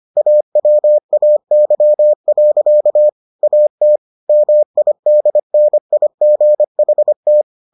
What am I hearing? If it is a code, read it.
Morse code, heard as a keyed tone, about 25 wpm, 600 Hz. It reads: AWAY. AT MIDNIGHT